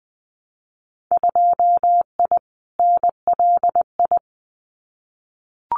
Morse code, heard as a keyed tone, about 20 wpm, 700 Hz. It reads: 2I NLI
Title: Winter Field Day Exchanges 20wpm